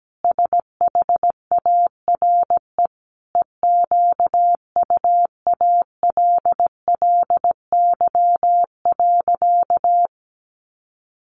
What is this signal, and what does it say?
Morse code, heard as a keyed tone, about 17 wpm, 700 Hz. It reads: SHARE EQUALLY.
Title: Sentences from Top 1400 Words 17wpm